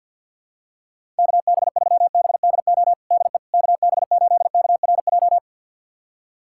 Morse code, heard as a keyed tone, about 51 wpm, 700 Hz. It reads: KB3BDO DE KB8KRJ
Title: USA Callsign DE Callsign 50wpm